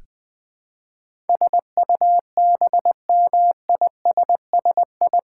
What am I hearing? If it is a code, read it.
Morse code, heard as a keyed tone, about 20 wpm, 700 Hz. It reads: SUBMISSI
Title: Build A Word - 4 Letters At A Time - 10-Letter Words - 20wpm